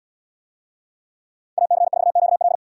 Morse code, heard as a keyed tone, about 77 wpm, 700 Hz. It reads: AC4QL